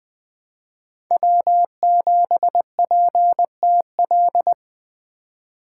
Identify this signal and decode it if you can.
Morse code, heard as a keyed tone, about 20 wpm, 700 Hz. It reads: W7PTL
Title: USA Call Signs - 5 Characters - 4x Character Spacing - 20wpm